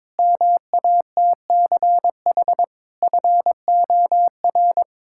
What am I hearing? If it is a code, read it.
Morse code, heard as a keyed tone, about 22 wpm, 700 Hz. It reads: MATCH FOR
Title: Sentences from Top 1600 Words 22wpm